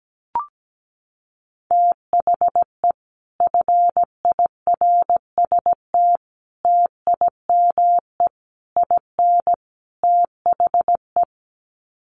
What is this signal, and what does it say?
Morse code, heard as a keyed tone, about 17 wpm, 700 Hz. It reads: THE FIRST TIME IN THE
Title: Sets of 5 Words Encore 17wpm